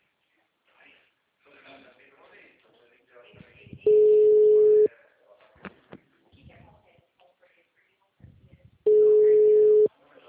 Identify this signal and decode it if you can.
Ringback tone (Europe, 425 Hz), with speech.